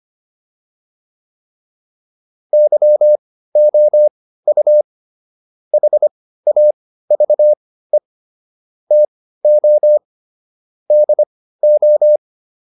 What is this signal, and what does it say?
Morse code, heard as a keyed tone, about 25 wpm, spaced as for 15 wpm, 600 Hz. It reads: YOU HAVE TO DO